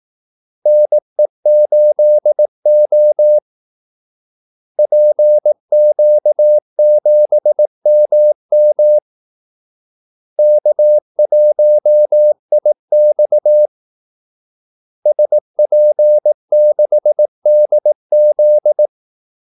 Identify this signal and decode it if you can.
Morse code, heard as a keyed tone, about 18 wpm, 600 Hz. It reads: NE8O PQ7MM K1IX SP6DZ
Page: Kaldesignaler 18 wpm